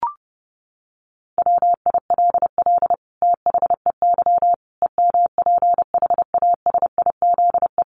Morse code, heard as a keyed tone, about 30 wpm, 700 Hz. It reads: WILL THEY EMPHASIZE
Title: Sentences from Top 1500 Words 30wpm